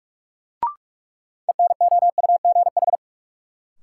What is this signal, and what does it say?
Morse code, heard as a keyed tone, about 46 wpm, 700 Hz. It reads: ENOUGH
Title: Sentences from Top 400 Words - New Word Drill - 45wpm